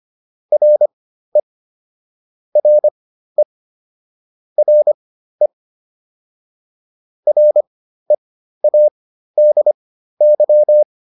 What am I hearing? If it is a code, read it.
Morse code, heard as a keyed tone, about 25 wpm, spaced as for 13 wpm, 600 Hz. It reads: RE RE RE READY